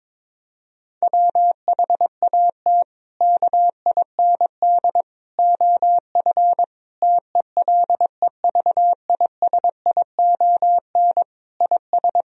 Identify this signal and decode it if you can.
Morse code, heard as a keyed tone, about 22 wpm, 700 Hz. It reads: WHAT KIND OF TELEVISION IS
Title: Sentences from Top 1000 Words - Review Part 4 - 22wpm